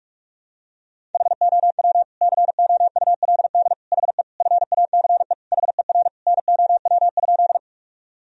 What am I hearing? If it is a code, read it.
Morse code, heard as a keyed tone, about 46 wpm, 700 Hz. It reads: HOW COULD HE FACE HER NOW?
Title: Sentences from Top 1000 Words - Review Part 2 - 45wpm